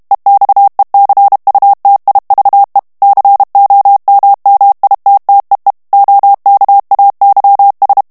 Morse code, heard as a keyed tone, about 32 wpm, 800 Hz. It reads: EXECUTIVE COMMITTEE OKAYS
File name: ANGLAIS texte clair 3 vitesse 30 mots minute